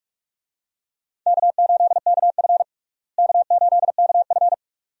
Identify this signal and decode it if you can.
Morse code, heard as a keyed tone, about 45 wpm, 700 Hz. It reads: K8KF K8KF
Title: USA Call Signs - 4 Characters - Repeated 2x - 45wpm